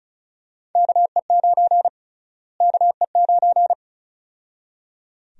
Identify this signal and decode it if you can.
Morse code, heard as a keyed tone, about 36 wpm, 700 Hz. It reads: KE9 KE9
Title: USA Call Sign Prefixes - Repeated 2x - 35wpm